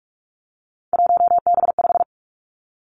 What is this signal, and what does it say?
Morse code, heard as a keyed tone, about 46 wpm, 700 Hz. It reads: 1B5
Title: Three Challenging Letter-Number Combinations 45wpm